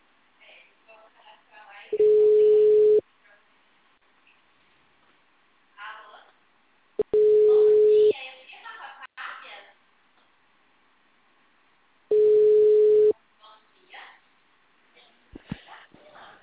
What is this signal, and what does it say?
Ringback tone (Europe, 425 Hz), with speech.
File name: out- -2015- - - .40